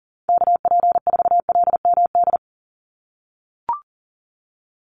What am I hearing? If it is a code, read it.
Morse code, heard as a keyed tone, about 38 wpm, 700 Hz. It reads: KP4LMD
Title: USA Call Signs 40wpm